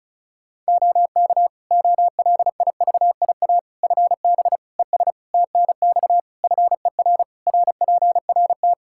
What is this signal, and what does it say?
Morse code, heard as a keyed tone, about 34 wpm, 700 Hz. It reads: OK OLIVIA FB ES TNX FER RPRT
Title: Practice QSO - Exchange 2 - Info - 35wpm